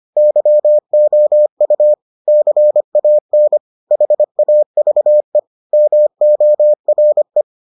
Morse code, heard as a keyed tone, about 25 wpm, 600 Hz. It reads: YOU CAN HAVE MORE